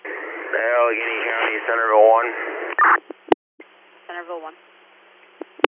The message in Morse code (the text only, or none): none